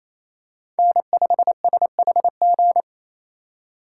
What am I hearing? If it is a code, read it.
Morse code, heard as a keyed tone, about 28 wpm, 700 Hz. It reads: N5SHG
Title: USA Call Signs - 5 Characters - 2x Character Spacing - 28wpm